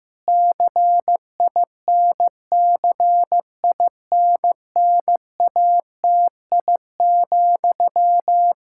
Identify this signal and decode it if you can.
Morse code, heard as a keyed tone, about 15 wpm, 700 Hz. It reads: CINCINNATI,
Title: Practice QSO - Exchange 1 - Beginning - 15wpm